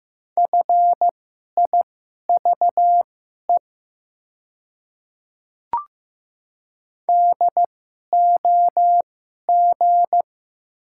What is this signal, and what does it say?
Morse code, heard as a keyed tone, about 15 wpm, spaced as for 11 wpm, 700 Hz. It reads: FIVE DOG
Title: 2x Character Spacing - Top 500 Words - 15wpm